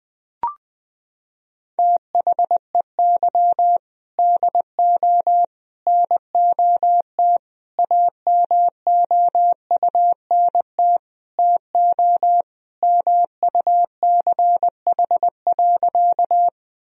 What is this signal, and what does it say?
Morse code, heard as a keyed tone, about 21 wpm, 700 Hz. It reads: THEY DO NOT AMOUNT TO MUCH.
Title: Sentences from Top 1000 Words - Review Part 4 - 20wpm